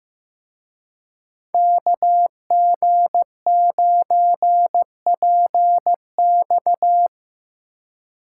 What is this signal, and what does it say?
Morse code, heard as a keyed tone, about 17 wpm, 700 Hz. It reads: KG9PX
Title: Speed Racing - USA 2x2 Call Signs 15wpm